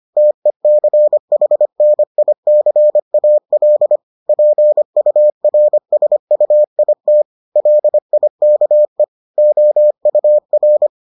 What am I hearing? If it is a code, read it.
Morse code, heard as a keyed tone, about 25 wpm, 600 Hz. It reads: TECHNICAL PURSUIT LIKE OUR